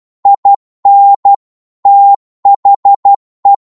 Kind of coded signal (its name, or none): Morse code